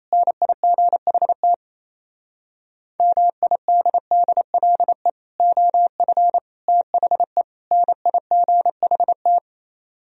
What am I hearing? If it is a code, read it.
Morse code, heard as a keyed tone, about 35 wpm, 690 Hz. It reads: NIGHT MIDDLE OF THE NIGHT